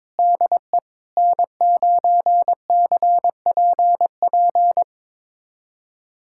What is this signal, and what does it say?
Morse code, heard as a keyed tone, about 22 wpm, 700 Hz. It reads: DE N9CPP
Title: USA Callsign DE Callsign - 5 Characters - 22wpm